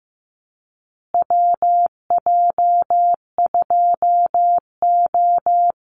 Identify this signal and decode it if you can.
Morse code, heard as a keyed tone, about 15 wpm, 700 Hz. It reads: WJ2O
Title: Call Signs Repeated 3x - 15wpm